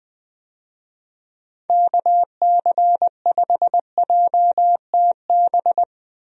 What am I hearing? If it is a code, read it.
Morse code, heard as a keyed tone, about 20 wpm, 700 Hz. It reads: KC5JTB
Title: USA Call Signs - 6 Characters - Repeated 3x - 20wpm